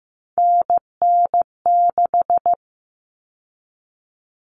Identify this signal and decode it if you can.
Morse code, heard as a keyed tone, about 15 wpm, 700 Hz. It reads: NN6
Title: Common Cut Numbers - 3-Digit 15wpm